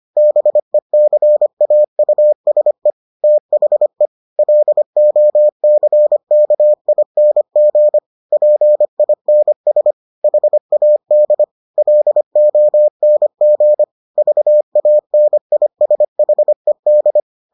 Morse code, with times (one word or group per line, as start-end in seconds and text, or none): 0.2-2.9 BECAUSE
3.2-4.1 THE
4.4-8.0 LOCKING
8.3-9.9 PINS
10.2-11.4 HAD
11.8-13.8 LONG
14.2-17.2 VANISHED